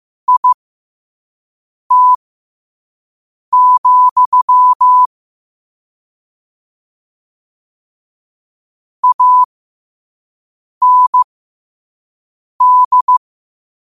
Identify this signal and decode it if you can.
Morse code, heard as a keyed tone, about 15 wpm, spaced as for 5 wpm, 1000 Hz. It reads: IT, AND
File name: qotdfiles / 5wpm